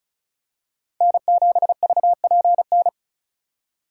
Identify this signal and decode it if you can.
Morse code, heard as a keyed tone, about 36 wpm, 700 Hz. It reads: N7VPN
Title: USA Call Signs - 5 Characters - 2x Character Spacing - 35wpm